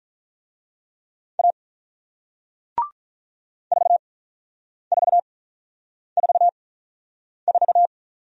Morse code, heard as a keyed tone, about 46 wpm, spaced as for 9 wpm, 700 Hz. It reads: A 4444